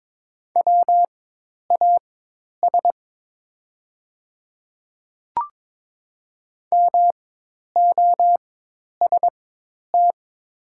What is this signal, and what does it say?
Morse code, heard as a keyed tone, about 22 wpm, spaced as for 10 wpm, 700 Hz. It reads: WAS MOST
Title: 4x Character Spacing - Top 100 Words - 22wpm